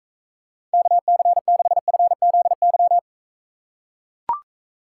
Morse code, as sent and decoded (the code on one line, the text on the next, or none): -.- -.-. -.... ..-. --.. -.--
KC6FZY